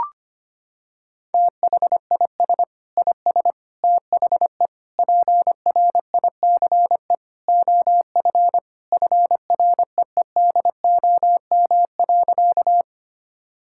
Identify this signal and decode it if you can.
Morse code, heard as a keyed tone, about 25 wpm, 700 Hz. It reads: THIS IS THE PRICE OF FREEDOM.
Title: Sentences from Top 1000 Words - Review Part 5 - 25wpm